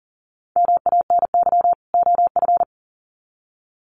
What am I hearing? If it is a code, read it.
Morse code, heard as a keyed tone, about 40 wpm, 700 Hz. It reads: MANY OF
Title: Sets of 2 Words 40wpm